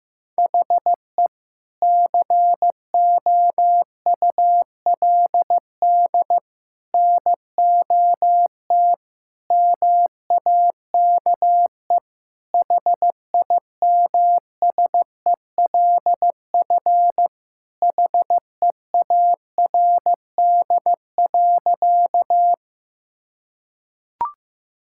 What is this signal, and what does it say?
Morse code, heard as a keyed tone, about 15 wpm, 700 Hz. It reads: HE COULD NOT MAKE HIMSELF HEARD.
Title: Sentences from Top 1600 Words 15wpm